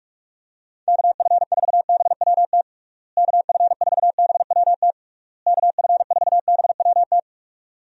Morse code, heard as a keyed tone, about 46 wpm, 700 Hz. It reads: KF4BWT KF4BWT KF4BWT
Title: USA Call Signs - 6 Characters - Repeated 3x - 45wpm